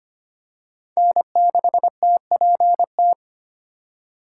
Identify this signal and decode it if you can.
Morse code, heard as a keyed tone, about 25 wpm, 700 Hz. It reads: N6TPT
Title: USA Call Signs - 5 Characters - 3x Character Spacing - 25wpm